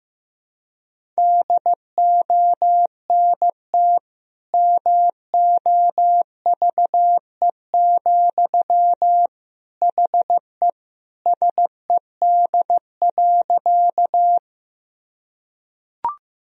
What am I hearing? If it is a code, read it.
Morse code, heard as a keyed tone, about 14 wpm, 700 Hz. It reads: DONT MOVE, HE SED.
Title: Sentences from Top 300 Words and Abbreviations 15wpm